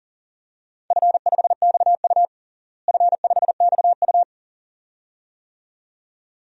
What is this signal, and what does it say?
Morse code, heard as a keyed tone, about 40 wpm, 700 Hz. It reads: F5XU F5XU
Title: DXCC-100 Call Signs - 4 Characters - Repeated 2x - 40wpm